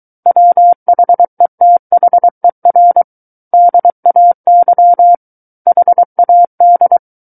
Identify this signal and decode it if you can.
Morse code, heard as a keyed tone, about 23 wpm, 700 Hz. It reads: WHETHER DAY HAD
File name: Book One / 14-Chapter 14 - 23 WPM